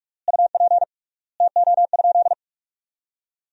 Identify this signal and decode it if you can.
Morse code, heard as a keyed tone, about 41 wpm, 690 Hz. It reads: UP TO?
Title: 3x Word-Spacing - Sentences from Top 100 Words - 45wpm